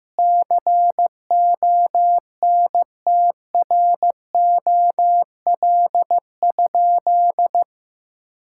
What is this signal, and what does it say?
Morse code, heard as a keyed tone, about 15 wpm, 700 Hz. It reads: CONTROL?
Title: Sentences from Top 1000 Words - Review Part 5 - 15wpm